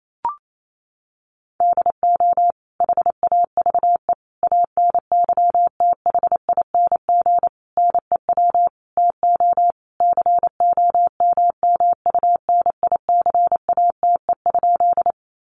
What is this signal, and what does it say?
Morse code, heard as a keyed tone, about 28 wpm, 700 Hz. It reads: DO HAVE ANYTHING NEW TO COMMUNICATE?
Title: Sentences from Top 1300 Words 28wpm